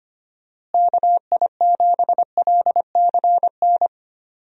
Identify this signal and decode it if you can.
Morse code, heard as a keyed tone, about 25 wpm, 700 Hz. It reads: KI7LCN
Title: USA Call Signs - 6 Characters - Repeated 2x - 25wpm